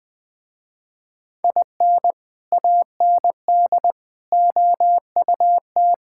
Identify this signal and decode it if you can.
Morse code, heard as a keyed tone, about 21 wpm, 700 Hz. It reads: IN AND OUT
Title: Sentences with 4-Word Phrases - Spoken Beginning - 20wpm